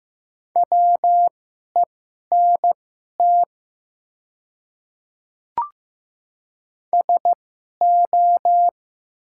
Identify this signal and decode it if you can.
Morse code, heard as a keyed tone, about 15 wpm, spaced as for 11 wpm, 700 Hz. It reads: WENT SO
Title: 2x Character Spacing - Top 200 Words - 15wpm